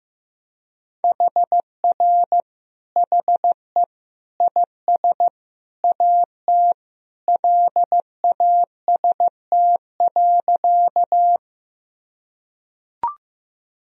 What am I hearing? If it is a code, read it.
Morse code, heard as a keyed tone, about 15 wpm, 700 Hz. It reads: HR HE IS AT LAST.
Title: Sentences from Top 300 Words and Abbreviations 15wpm